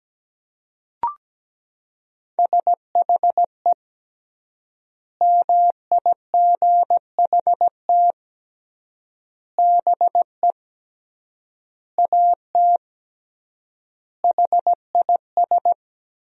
Morse code, heard as a keyed tone, about 17 wpm, 700 Hz. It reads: SHE MIGHT BE AT HIS
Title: 3x Word-Spacing - Sentences from Top 300 Words - 17wpm